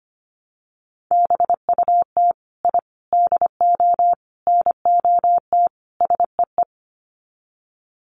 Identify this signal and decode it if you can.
Morse code, heard as a keyed tone, about 25 wpm, 700 Hz. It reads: BUT I DO NOT SEE
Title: Sets of 5 Words Encore 25wpm